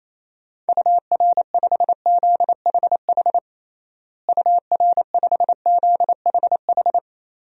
Morse code, heard as a keyed tone, about 27 wpm, 700 Hz. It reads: UR5ZHH UR5ZHH
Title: DXCC-100 Call Signs - 6 Characters - Repeated 3x - 28wpm